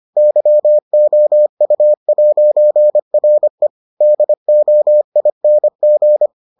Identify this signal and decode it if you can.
Morse code, heard as a keyed tone, about 25 wpm, 600 Hz. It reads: YOU'RE DOING